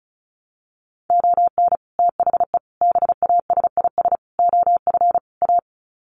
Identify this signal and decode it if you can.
Morse code, heard as a keyed tone, about 34 wpm, 700 Hz. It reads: ON THE BASIS OF A
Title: Sets of 5 Words Encore 35wpm